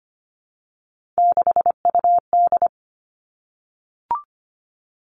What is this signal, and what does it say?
Morse code, heard as a keyed tone, about 25 wpm, 700 Hz. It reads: 6UD
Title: Three Challenging Letter-Number Combinations 25wpm